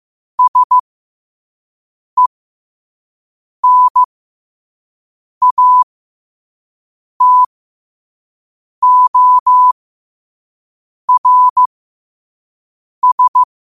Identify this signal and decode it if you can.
Morse code, heard as a keyed tone, about 15 wpm, spaced as for 5 wpm, 1000 Hz. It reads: SENATORS
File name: qotdfiles / 5wpm